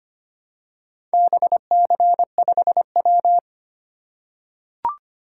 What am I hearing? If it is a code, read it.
Morse code, heard as a keyed tone, about 25 wpm, 700 Hz. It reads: BC5W
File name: practice / 25wpm / Four Letter-Number Combinations - 25wpm